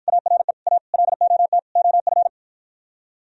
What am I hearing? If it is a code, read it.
Morse code, heard as a keyed tone, about 52 wpm, 700 Hz. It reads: ARE A LOT OF